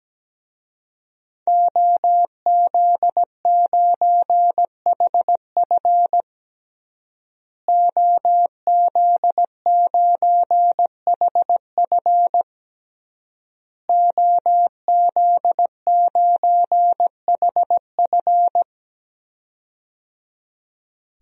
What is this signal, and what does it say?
Morse code, heard as a keyed tone, about 17 wpm, 700 Hz. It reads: OZ9HF OZ9HF OZ9HF
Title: DXCC-100 Call Signs - 5 Characters - Repeated 3x - 17wpm